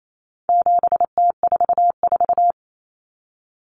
Morse code, heard as a keyed tone, about 28 wpm, 700 Hz. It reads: 7T44
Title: Common Cut Numbers - 4-Digit 28wpm